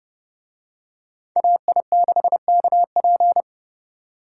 Morse code, heard as a keyed tone, about 29 wpm, 700 Hz. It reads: AI6KP